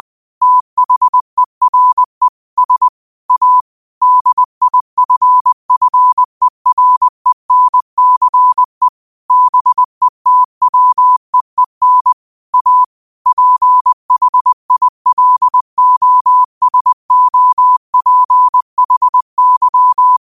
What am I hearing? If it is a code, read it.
Morse code, heard as a keyed tone, about 20 wpm, 1000 Hz. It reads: THERE S A DIFFERENCE BETWEEN A PHILOSOPHY